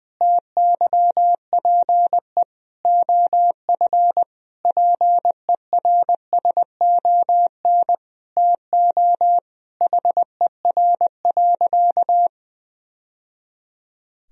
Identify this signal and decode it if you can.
Morse code, heard as a keyed tone, about 21 wpm, 700 Hz. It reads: TYPE OF PERSON TO HER.
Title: Sentences from Top 1000 Words - Review Part 3 - 20wpm